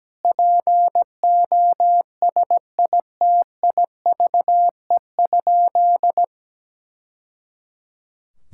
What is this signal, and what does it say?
Morse code, heard as a keyed tone, about 17 wpm, 700 Hz. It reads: POSITIVE?
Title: Sentences from Top 1000 Words - Review Part 5 - 17wpm